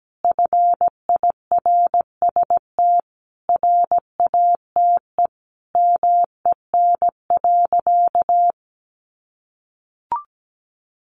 Morse code, heard as a keyed tone, about 17 wpm, 700 Hz. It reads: FIRST RATE MEN.
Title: Sentences from Top 1300 Words 17wpm